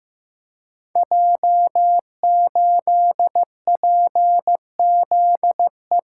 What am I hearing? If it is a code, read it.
Morse code, heard as a keyed tone, about 15 wpm, 700 Hz. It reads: J8PZE
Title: Code Groups - 5 Letter:Numbers - 15wpm